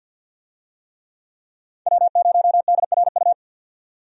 Morse code, heard as a keyed tone, about 51 wpm, 700 Hz. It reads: W0DRU
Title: USA Call Signs - 5 Characters - 50wpm